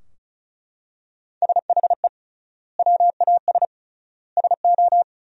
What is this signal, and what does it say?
Morse code, heard as a keyed tone, about 36 wpm, 700 Hz. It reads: SHE WAS SO
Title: 3x Word-Spacing - Sentences from Top 400 Words - 35wpm